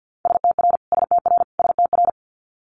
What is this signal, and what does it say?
Morse code, heard as a keyed tone, about 50 wpm, 700 Hz. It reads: STR STR STR
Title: Multi-letter Phonemes Repeated 3x 50wpm